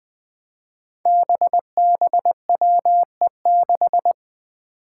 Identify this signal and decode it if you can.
Morse code, heard as a keyed tone, about 21 wpm, 700 Hz. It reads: BBWE6
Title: Five Challenging Letter-Number Combinations 20wpm